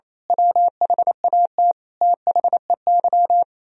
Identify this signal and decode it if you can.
Morse code, heard as a keyed tone, about 26 wpm, 700 Hz. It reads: WHAT THEY
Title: Speed-Racing - Sets of 5 Words 28wpm